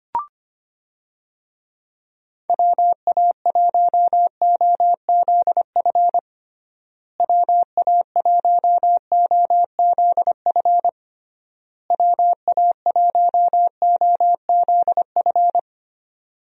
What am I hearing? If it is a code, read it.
Morse code, heard as a keyed tone, about 25 wpm, 700 Hz. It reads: WA1OZF WA1OZF WA1OZF
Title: USA Call Signs - 6 Characters - Repeated 3x - 25wpm